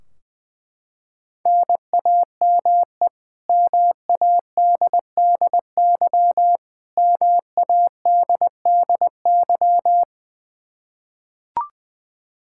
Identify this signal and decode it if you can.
Morse code, heard as a keyed tone, about 20 wpm, 700 Hz. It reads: NAME MADDY MADDY
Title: Practice QSO - Exchange 1 - Beginning - 20wpm